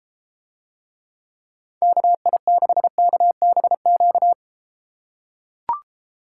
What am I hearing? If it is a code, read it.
Morse code, heard as a keyed tone, about 32 wpm, 700 Hz. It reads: KI6KBQ